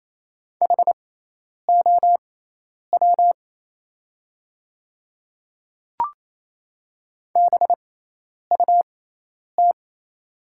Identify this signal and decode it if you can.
Morse code, heard as a keyed tone, about 27 wpm, spaced as for 10 wpm, 700 Hz. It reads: HOW BUT
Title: 6x Character Spacing - Top 300 Words - 28wpm